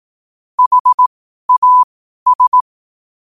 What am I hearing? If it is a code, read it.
Morse code, heard as a keyed tone, about 18 wpm, spaced as for 12 wpm, 1000 Hz. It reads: HAS
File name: qotdfiles / 12wpm